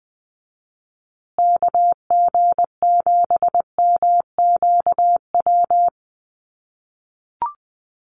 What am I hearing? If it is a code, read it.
Morse code, heard as a keyed tone, about 20 wpm, 700 Hz. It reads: KG7MQW
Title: USA Call Signs Repeated 2x 20wpm